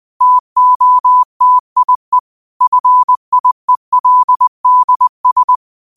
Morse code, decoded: TOTIE FIELDS

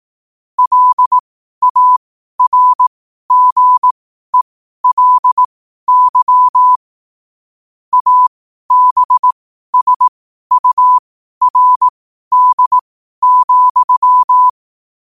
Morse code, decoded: LARGELY ABSURD,

